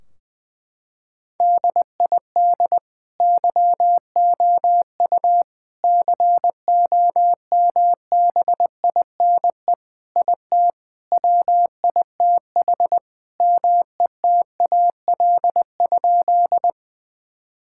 Morse code, heard as a keyed tone, about 20 wpm, 700 Hz. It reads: DID YOU COMBINE IT WITH METAL?
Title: Sentences from Top 1600 Words 20wpm